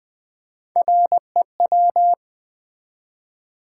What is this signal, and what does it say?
Morse code, heard as a keyed tone, about 20 wpm, 700 Hz. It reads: REW
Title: Build A Word - 3 Letters At A Time - 6-Letter Words - 20wpm